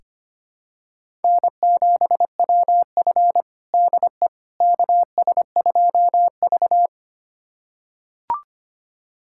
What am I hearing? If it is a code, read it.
Morse code, heard as a keyed tone, about 25 wpm, 700 Hz. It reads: N7WF DE KS2V
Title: USA Callsign DE Callsign - 4 Characters - 25wpm